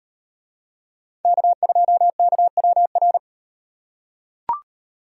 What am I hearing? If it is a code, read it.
Morse code, heard as a keyed tone, about 38 wpm, 700 Hz. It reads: K2KWR